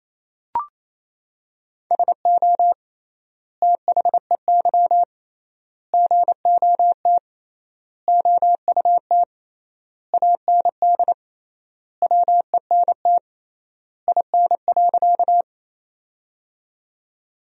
Morse code, heard as a keyed tone, about 28 wpm, 700 Hz. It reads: SO THEY GOT OUT AND WENT IN.
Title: 3x Word-Spacing - Sentences from Top 300 Words - 28wpm